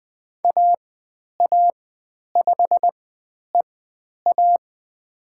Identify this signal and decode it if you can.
Morse code, heard as a keyed tone, about 20 wpm, spaced as for 10 wpm, 700 Hz. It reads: AA5EA
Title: 102 Lesson 17g - Letter u - Call Signs 20wpm